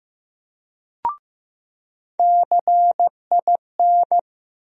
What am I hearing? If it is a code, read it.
Morse code, heard as a keyed tone, about 15 wpm, 700 Hz. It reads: CIN
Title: USA Call Sign Postfixes - 3 Letters 15wpm